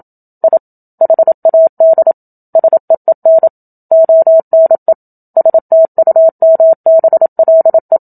Morse code, heard as a keyed tone, about 27 wpm, 650 Hz. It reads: I HAD SEEN ONE STUMBLE
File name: Book Two / 01-Chapter 1 - 27 WPM